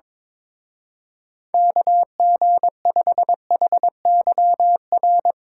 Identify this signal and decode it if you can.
Morse code, heard as a keyed tone, about 24 wpm, 700 Hz. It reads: KG5HYR
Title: Speed Racing - USA 2x3 Call Signs 22wpm